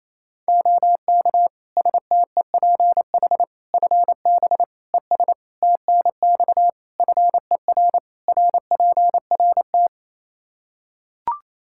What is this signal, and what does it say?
Morse code, heard as a keyed tone, about 28 wpm, 700 Hz. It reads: OK STEPH FB ES TNX FER RPRT
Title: Practice QSO - Exchange 2 - Info - 28wpm